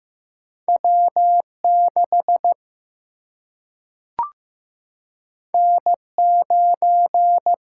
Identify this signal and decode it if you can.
Morse code, heard as a keyed tone, about 15 wpm, 700 Hz. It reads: W 6 N 9
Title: Single-Letter Single-Number 15wpm